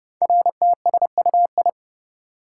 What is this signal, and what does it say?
Morse code, heard as a keyed tone, about 29 wpm, 700 Hz. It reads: RTSUI
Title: Code Groups - 5 Letters - 30wpm